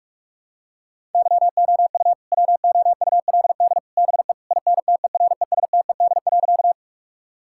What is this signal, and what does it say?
Morse code, heard as a keyed tone, about 41 wpm, 700 Hz. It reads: YOU WOULD BE INTERESTED.
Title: Sentences from Top 1000 Words - Review Part 6 - 45wpm